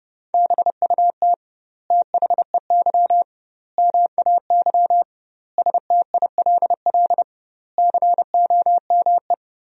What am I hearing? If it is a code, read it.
Morse code, heard as a keyed tone, about 30 wpm, 700 Hz. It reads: BUT THEY MAY STILL COME
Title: 2x Word-Spacing - Sentences from Top 300 Words - 30wpm